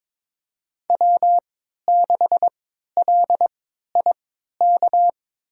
Morse code, heard as a keyed tone, about 22 wpm, spaced as for 13 wpm, 700 Hz. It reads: W6LIK
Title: USA Call Signs - 3x Character Spacing - 22wpm